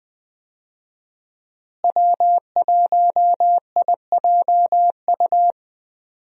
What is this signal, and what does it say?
Morse code, heard as a keyed tone, about 20 wpm, 700 Hz. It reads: W1IJU
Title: USA Call Signs - 5 Characters - Repeated 3x - 20wpm